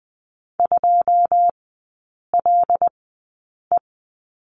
Morse code, heard as a keyed tone, about 20 wpm, spaced as for 8 wpm, 700 Hz. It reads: 2LE
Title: 100 Lesson 17e - Letter u - Three Characters 20wpm